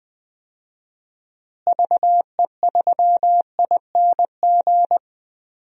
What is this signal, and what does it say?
Morse code, heard as a keyed tone, about 20 wpm, 700 Hz. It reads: VE3ING
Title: DXCC-100 Call Signs - 6 Characters - Repeated 3x - 20wpm